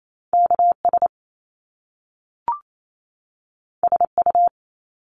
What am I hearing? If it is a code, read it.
Morse code, heard as a keyed tone, about 28 wpm, 700 Hz. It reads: K S S U
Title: Top Two Letter Combinations in Words 28wpm